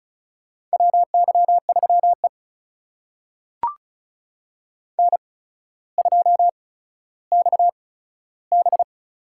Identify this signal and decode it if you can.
Morse code, heard as a keyed tone, about 36 wpm, spaced as for 15 wpm, 700 Hz. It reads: WY3E N2XB
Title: USA Call Signs - 4 Characters - 8x Character Spacing - 35wpm